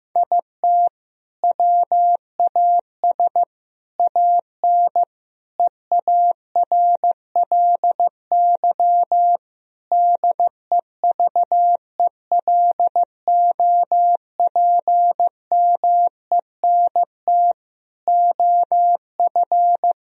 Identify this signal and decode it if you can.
Morse code, heard as a keyed tone, about 15 wpm, 700 Hz. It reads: IT WAS AN EARLY DEVELOPMENT OF
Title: Sentences from Top 1000 Words - Review Part 4 - 15wpm